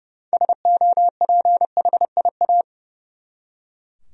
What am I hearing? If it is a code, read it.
Morse code, heard as a keyed tone, about 30 wpm, 700 Hz. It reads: SOPHIA
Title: USA Common Names - 30wpm